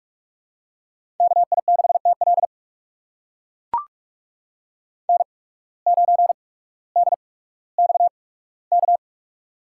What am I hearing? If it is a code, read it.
Morse code, heard as a keyed tone, about 44 wpm, spaced as for 19 wpm, 690 Hz. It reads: KI6TL N9DXK
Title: USA Call Signs - 5 Characters - 8x Character Spacing - 45wpm